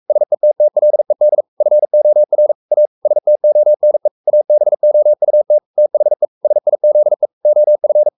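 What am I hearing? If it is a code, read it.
Morse code, heard as a keyed tone, about 44 wpm, 600 Hz. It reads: SETTLED FOR A STONE ABOUT THE SIZE OF
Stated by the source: Morse code at 43 wpm sending Ken Follett's novel Never in English, from CW med Gnister.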